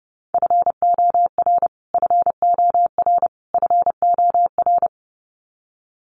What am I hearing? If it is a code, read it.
Morse code, heard as a keyed tone, about 30 wpm, 700 Hz. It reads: FOR FOR FOR
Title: Top 200 Words Repeated 30wpm